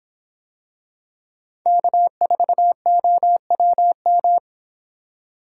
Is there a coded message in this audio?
Yes: Morse code from 1.7 to 4.4 s, heard as a keyed tone, about 25 wpm, 700 Hz. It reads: K4OWM